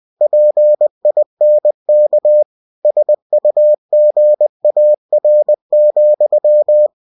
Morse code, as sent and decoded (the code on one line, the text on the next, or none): .--. .. -. -.- / ... ..- --. .- .-. --..--
PINK SUGAR,